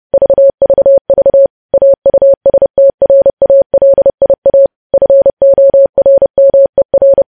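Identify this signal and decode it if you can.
Morse code, heard as a keyed tone, about 30 wpm, 550 Hz. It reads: VVV AUSTRALIA FORMER
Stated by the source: Morse news headlines sent at 30 wpm.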